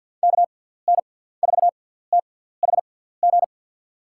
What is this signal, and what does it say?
Morse code, heard as a keyed tone, about 46 wpm, spaced as for 17 wpm, 700 Hz. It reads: KN4THG
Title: USA Call Signs - 6 Characters - 6x Character Spacing - 50wpm